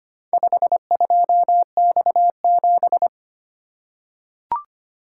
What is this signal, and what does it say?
Morse code, heard as a keyed tone, about 25 wpm, 700 Hz. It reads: 52X7
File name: practice / 25wpm / Four Letter-Number Combinations - 25wpm